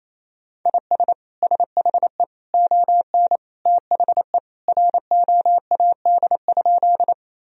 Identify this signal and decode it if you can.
Morse code, heard as a keyed tone, about 28 wpm, 700 Hz. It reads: IS SHE ON THE ROAD?
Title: Sentences from Top 1000 Words - Review Part 2 - 28wpm